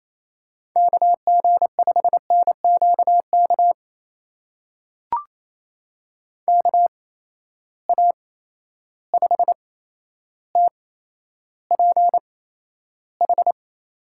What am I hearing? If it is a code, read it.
Morse code, heard as a keyed tone, about 27 wpm, spaced as for 12 wpm, 700 Hz. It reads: KG5NQK KA5TPH
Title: USA Call Signs - 6 Characters - 8x Character Spacing - 28wpm